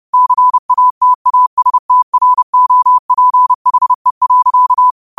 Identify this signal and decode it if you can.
Morse code, heard as a keyed tone, about 30 wpm, 1000 Hz. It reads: CATASTROPHE.
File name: qotdfiles / 30wpm